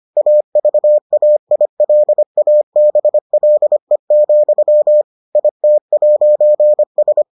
Morse code, heard as a keyed tone, about 25 wpm, 600 Hz. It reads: AVAILABLE, IT'S